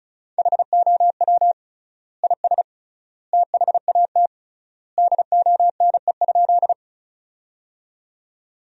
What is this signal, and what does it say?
Morse code, heard as a keyed tone, about 36 wpm, 700 Hz. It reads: HOW IS THAT DONE?
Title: 3x Word-Spacing - Sentences from Top 500 Words - 35wpm